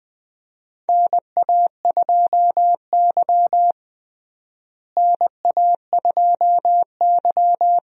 Morse code, heard as a keyed tone, about 20 wpm, 700 Hz. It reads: NA2Y NA2Y
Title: USA Call Signs - 4 Characters - Repeated 3x - 20wpm